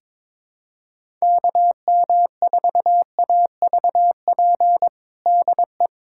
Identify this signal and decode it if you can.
Morse code, heard as a keyed tone, about 22 wpm, 700 Hz. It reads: KM4AVP DE
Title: USA Callsign DE Callsign - 6 Characters - 22wpm